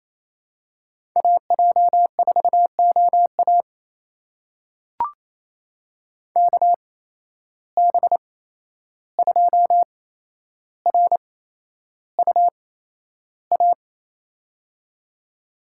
Morse code, heard as a keyed tone, about 28 wpm, spaced as for 11 wpm, 700 Hz. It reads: AJ4OA KB2RUA
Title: USA Call Signs - 8x Character Spacing - 28wpm